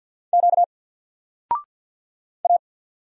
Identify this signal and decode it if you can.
Morse code, heard as a keyed tone, about 48 wpm, 700 Hz. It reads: Q A
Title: Single Letter-Number 50wpm